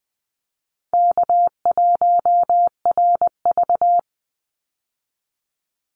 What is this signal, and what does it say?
Morse code, heard as a keyed tone, about 20 wpm, 700 Hz. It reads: K1RV
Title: Easy Call Signs - 20wpm